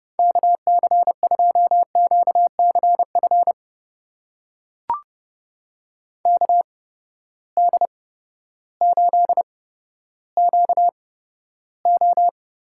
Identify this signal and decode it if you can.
Morse code, heard as a keyed tone, about 31 wpm, spaced as for 14 wpm, 700 Hz. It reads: KC2QCF KD8QO
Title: USA Call Signs - 6 Characters - 8x Character Spacing - 30wpm